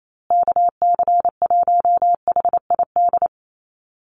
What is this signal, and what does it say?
Morse code, heard as a keyed tone, about 28 wpm, 700 Hz. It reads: KC1HID
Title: USA Call Signs Repeated 3x 28wpm